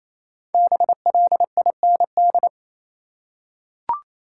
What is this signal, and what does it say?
Morse code, heard as a keyed tone, about 28 wpm, 700 Hz. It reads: BLIND
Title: Five Letter Words - 28wpm